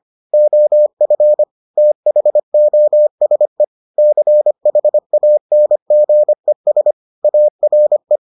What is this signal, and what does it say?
Morse code, heard as a keyed tone, about 25 wpm, 600 Hz. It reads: OF THOSE CHANGES ARE